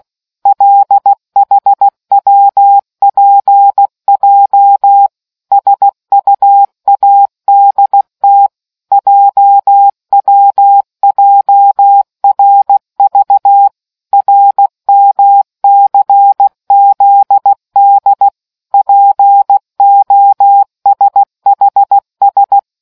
Morse code, heard as a keyed tone, about 16 wpm, 800 Hz. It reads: LHWPJ SUADT JWJRV RMCZD POSHS